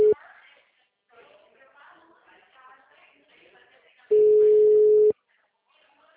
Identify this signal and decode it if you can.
Ringback tone (Europe, 425 Hz), with speech.